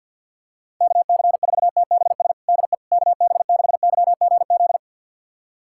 Morse code, heard as a keyed tone, about 48 wpm, 700 Hz. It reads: KC4TBS DE KB6YGZ
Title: USA Callsign DE Callsign - 6 Characters - 50wpm